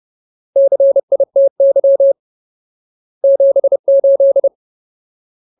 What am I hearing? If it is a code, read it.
Morse code, heard as a keyed tone, about 30 wpm, 550 Hz. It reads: CITY 78
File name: Ultimate Word List / 30WPM Ultimate